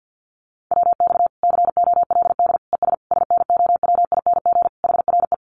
Morse code, heard as a keyed tone, about 50 wpm, 700 Hz. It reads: WX COLD ES SNOWING HRE
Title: Common QSO Phrases 50wpm